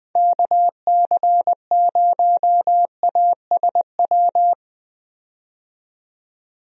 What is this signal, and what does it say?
Morse code, heard as a keyed tone, about 21 wpm, 700 Hz. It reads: KC0ASW
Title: USA Call Signs - 6 Characters - 20wpm